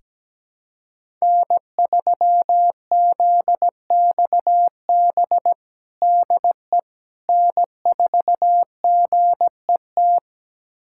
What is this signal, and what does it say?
Morse code, heard as a keyed tone, about 17 wpm, 700 Hz. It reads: N3ZXB DE N4GET
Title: USA Callsign DE Callsign - 5 Characters - 17wpm